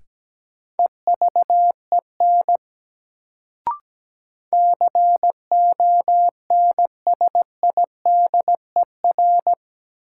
Morse code, heard as a keyed tone, about 17 wpm, 700 Hz. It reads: EVEN CONSIDER
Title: Sentences from Top 600 Words - New Word Drill - 17wpm